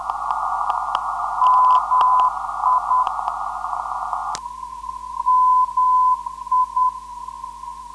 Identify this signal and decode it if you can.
Morse code, heard as a keyed tone, about 10 wpm, 1010 Hz, with noise. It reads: MI MI
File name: ndb zvok / MI-355khz